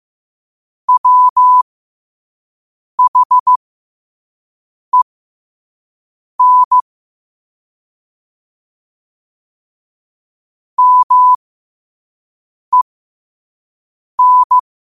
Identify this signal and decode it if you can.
Morse code, heard as a keyed tone, about 15 wpm, spaced as for 5 wpm, 1000 Hz. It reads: WHEN MEN
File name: qotdfiles / 5wpm